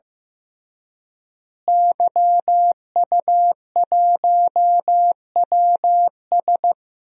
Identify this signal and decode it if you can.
Morse code, heard as a keyed tone, about 15 wpm, 700 Hz. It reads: YU1WS
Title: DXCC-100 Call Signs - 5 Characters - Repeated 2x - 15wpm